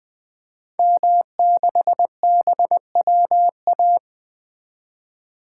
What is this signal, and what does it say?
Morse code, heard as a keyed tone, about 20 wpm, 700 Hz. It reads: M6BWA
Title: 2023 SOTA Activator Call Signs - 20wpm